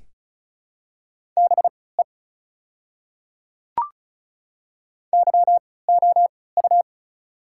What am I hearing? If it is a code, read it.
Morse code, heard as a keyed tone, about 36 wpm, spaced as for 20 wpm, 700 Hz. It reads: BE YOU
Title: 3x Character Spacing - Top 200 Words - 35wpm